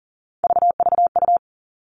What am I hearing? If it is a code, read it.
Morse code, heard as a keyed tone, about 42 wpm, 700 Hz. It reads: VVU
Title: Three Challenging Letter-Number Combinations 40wpm